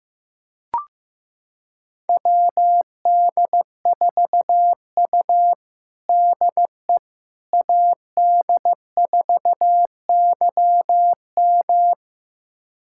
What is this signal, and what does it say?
Morse code, heard as a keyed tone, about 15 wpm, 700 Hz. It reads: WD4U DE AD4YM
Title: USA Callsign DE Callsign 15wpm